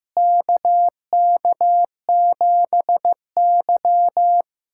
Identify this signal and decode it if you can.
Morse code, heard as a keyed tone, about 16 wpm, 700 Hz. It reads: KK7Y
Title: USA Callsign DE Callsign - 4 Characters - 15wpm